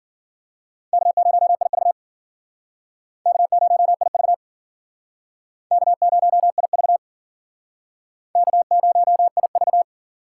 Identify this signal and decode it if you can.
Morse code, heard as a keyed tone, about 51 wpm, 700 Hz. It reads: K0IV K0IV K0IV K0IV